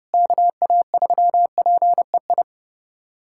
Morse code, heard as a keyed tone, about 30 wpm, 700 Hz. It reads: KA3PEI
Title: USA Callsign DE Callsign - 6 Characters - 30wpm